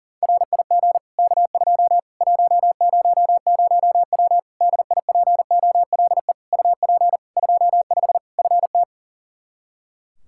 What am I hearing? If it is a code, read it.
Morse code, heard as a keyed tone, about 40 wpm, 700 Hz. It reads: RIG K2 100W DIPOLE UP 25 FT
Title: Practice QSO - Exchange 2 - Info - 40wpm